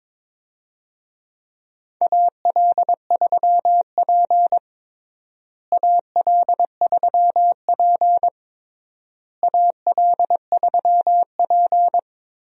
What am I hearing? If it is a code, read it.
Morse code, heard as a keyed tone, about 22 wpm, 700 Hz. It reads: AL3P AL3P AL3P
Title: USA Call Signs - 4 Characters - Repeated 3x - 22wpm